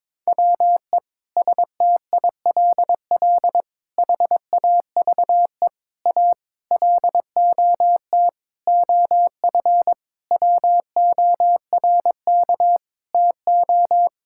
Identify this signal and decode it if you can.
Morse code, heard as a keyed tone, about 22 wpm, 700 Hz. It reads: WE STILL HAVE A LOT OF WORK TO
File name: practice / 22wpm / Sentences with 8-Word Phrases - 22wpm